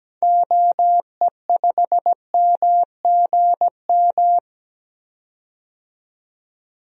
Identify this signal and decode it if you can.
Morse code, heard as a keyed tone, about 17 wpm, 700 Hz. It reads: OE5MGM
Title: DXCC-100 Call Signs - 6 Characters - 17wpm